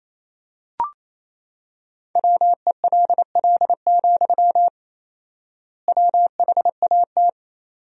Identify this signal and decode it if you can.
Morse code, heard as a keyed tone, about 28 wpm, 700 Hz. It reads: WELL, WHAT
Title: 4x Word-Spacing - Sentences from Top 500 Words - 28wpm